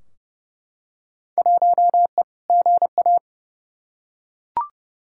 Morse code, heard as a keyed tone, about 32 wpm, 700 Hz. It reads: 1E GA
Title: Field Day Exchanges - 30wpm